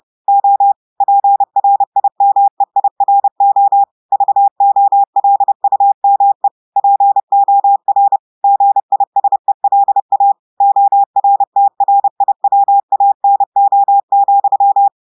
Morse code, heard as a keyed tone, about 30 wpm, 800 Hz. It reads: O PRIMEIRO VOLUME POR GISELA ORTRIWANO,
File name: audios / portugais / portugais texte en clair 1 vitesse 30 mots minutes